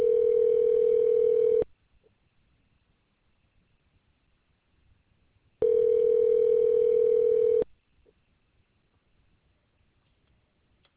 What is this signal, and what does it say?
Ringback tone (North America), 440 + 480 Hz.